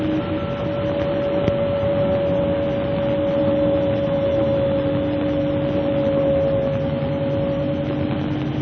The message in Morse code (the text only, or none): none